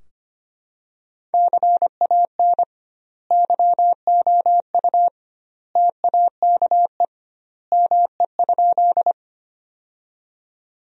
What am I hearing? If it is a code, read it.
Morse code, heard as a keyed tone, about 25 wpm, 700 Hz. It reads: CAN YOU TAKE ME?
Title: 2x Word-Spacing - Sentences from Top 200 Words - 25wpm